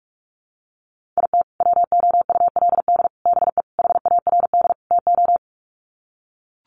Morse code, heard as a keyed tone, about 44 wpm, 700 Hz. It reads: IT WOULD BE HARD TO
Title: Sets of 5 Words Encore2 45wpm